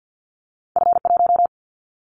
Morse code, heard as a keyed tone, about 50 wpm, 700 Hz. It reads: F1
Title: Callsign Prefixes for Top 25 DXCC - 50wpm